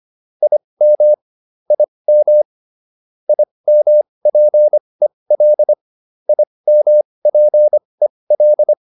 Morse code, heard as a keyed tone, about 25 wpm, spaced as for 20 wpm, 600 Hz. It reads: IM IM IMPEL IMPEL